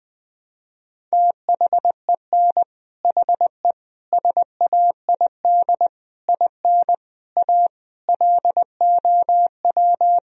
Morse code, heard as a keyed tone, about 20 wpm, 700 Hz. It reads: THEN HE SAID IN A LOW
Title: Sentences from Top 1000 Words - Review Part 1 - 20wpm